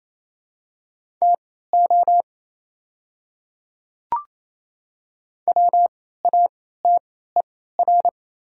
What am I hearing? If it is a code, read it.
Morse code, heard as a keyed tone, about 28 wpm, spaced as for 16 wpm, 700 Hz. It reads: TO WATER
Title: 3x Character Spacing - Top 100 Words - 28wpm